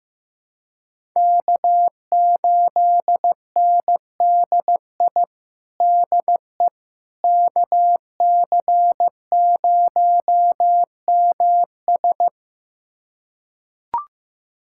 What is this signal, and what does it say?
Morse code, heard as a keyed tone, about 15 wpm, 700 Hz. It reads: K8NDI DE KC0MS
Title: USA Callsign DE Callsign - 5 Characters - 15wpm